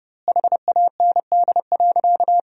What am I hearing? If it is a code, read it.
Morse code, heard as a keyed tone, about 30 wpm, 700 Hz. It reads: HAND.
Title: 2x Word-Spacing - Sentences from Top 400 Words - 30wpm